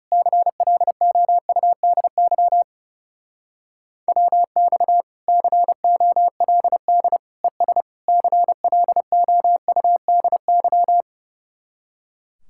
Morse code, heard as a keyed tone, about 34 wpm, 700 Hz. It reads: CLOUDY WX COLD ES CLOUDY
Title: Speed-Racing Common QSO Phrases Encore - 30wpm